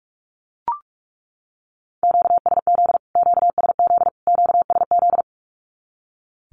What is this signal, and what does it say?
Morse code, heard as a keyed tone, about 44 wpm, 700 Hz. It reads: QSZ QSZ QSZ
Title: Q-Codes Repeated 3x 45wpm